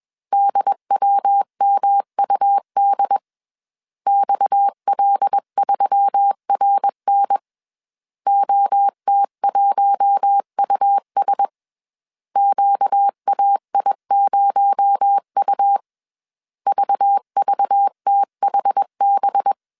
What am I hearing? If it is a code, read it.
Morse code, heard as a keyed tone, about 21 wpm, 800 Hz. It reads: DWMUD XL3RN OT1US QAI0U VVTHB
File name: CW / DET / 1 6 1500C S A 20wpm 2048